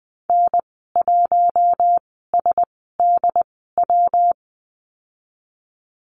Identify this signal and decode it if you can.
Morse code, heard as a keyed tone, about 19 wpm, spaced as for 14 wpm, 700 Hz. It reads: N1SDW
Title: 122 Lesson 20g - Number 3 - Call Signs 20wpm